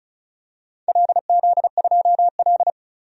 Morse code, heard as a keyed tone, about 36 wpm, 700 Hz. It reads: LZ2L
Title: DXCC-100 Call Signs - 4 Characters - 35wpm